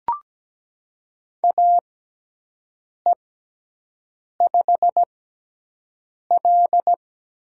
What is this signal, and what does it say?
Morse code, heard as a keyed tone, about 17 wpm, spaced as for 6 wpm, 700 Hz. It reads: AE5L
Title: USA Call Signs - 6x Character Spacing - 17wpm